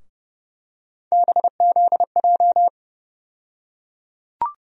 Morse code, heard as a keyed tone, about 29 wpm, 700 Hz. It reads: BZJ
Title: USA Call Sign Postfixes - 3 Letters 30wpm